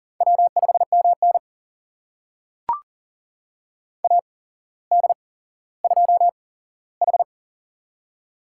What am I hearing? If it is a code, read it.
Morse code, heard as a keyed tone, about 42 wpm, spaced as for 17 wpm, 690 Hz. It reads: W5MN AD2H
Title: USA Call Signs - 4 Characters - 8x Character Spacing - 40wpm